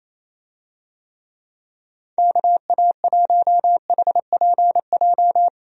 Morse code, heard as a keyed tone, about 27 wpm, 700 Hz. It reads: KA1HPJ
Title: USA Call Signs - 6 Characters - 28wpm